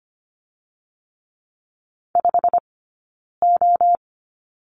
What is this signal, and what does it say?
Morse code, heard as a keyed tone, about 25 wpm, 700 Hz. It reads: 5 O
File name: Lesson 15d - Number 5 - Two Characters 25wpm